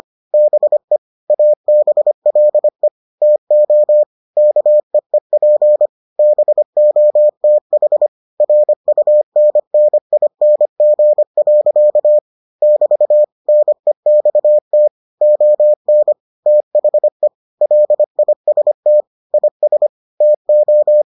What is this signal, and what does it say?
Morse code, heard as a keyed tone, about 25 wpm, 600 Hz. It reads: BE ABLE TO KEEP BOTH RUNNING. <BT> NEXT ON THE LIST IS TO